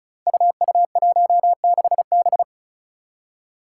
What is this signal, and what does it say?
Morse code, heard as a keyed tone, about 36 wpm, 700 Hz. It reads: UU16B
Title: Five Challenging Letter-Number Combinations 35wpm